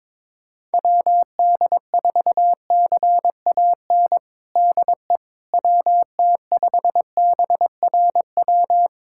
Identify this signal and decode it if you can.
Morse code, heard as a keyed tone, about 22 wpm, 700 Hz. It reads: WD4CAN DE WT5BRW
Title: USA Callsign DE Callsign 22wpm